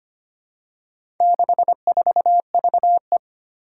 Morse code, heard as a keyed tone, about 25 wpm, 700 Hz. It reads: 64VE
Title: Four Challenging Letter-Number Combinations 25wpm